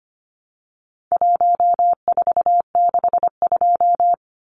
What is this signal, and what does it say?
Morse code, heard as a keyed tone, about 25 wpm, 700 Hz. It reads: 1462
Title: Numbers - 4-Digit 25wpm